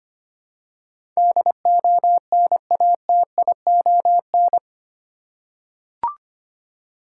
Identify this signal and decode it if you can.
Morse code, heard as a keyed tone, about 25 wpm, 700 Hz. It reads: DONATION
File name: 8 Letter Words - First 3 Letters Given - 25wpm